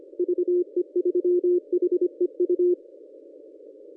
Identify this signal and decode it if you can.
Morse code, heard as a keyed tone, about 25 wpm, 360 Hz. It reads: VE3HEU
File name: contests / ss-cw-k5tr-op-n5rz -1000